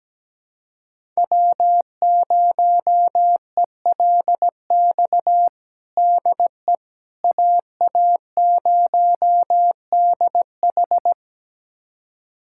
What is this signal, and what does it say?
Morse code, heard as a keyed tone, about 17 wpm, 700 Hz. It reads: W0ELX DE AA0DH
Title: USA Callsign DE Callsign - 5 Characters - 17wpm